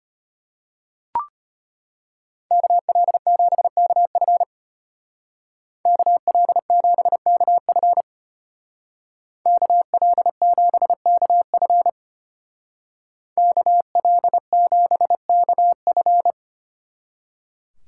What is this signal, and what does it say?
Morse code, heard as a keyed tone, about 33 wpm, 700 Hz. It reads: KL7KF KL7KF KL7KF KL7KF